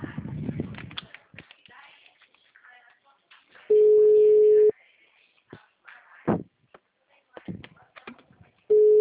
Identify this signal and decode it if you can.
Ringback tone (Europe, 425 Hz), with speech.